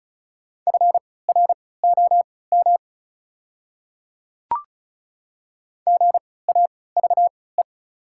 Morse code, heard as a keyed tone, about 36 wpm, spaced as for 20 wpm, 700 Hz. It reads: FROM GAVE
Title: 3x Character Spacing - Top 500 Words - 35wpm